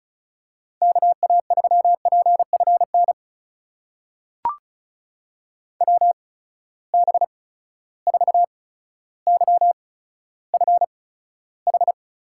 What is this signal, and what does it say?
Morse code, heard as a keyed tone, about 36 wpm, spaced as for 15 wpm, 700 Hz. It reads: KA3PFN WB4YFH
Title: USA Call Signs - 6 Characters - 8x Character Spacing - 35wpm